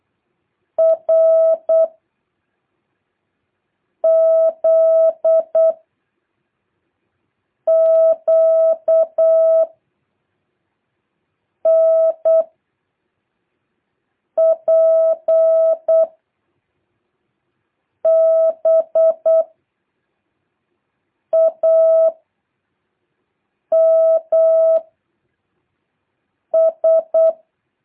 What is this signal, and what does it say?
Morse code, heard as a keyed tone, about 8 wpm, 650 Hz. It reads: R Z Q N P B A M S
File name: jh8cbh / 2023cw